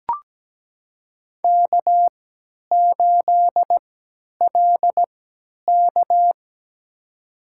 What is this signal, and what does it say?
Morse code, heard as a keyed tone, about 17 wpm, spaced as for 10 wpm, 700 Hz. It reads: K8LK
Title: USA Call Signs - 4 Characters - 3x Character Spacing - 17wpm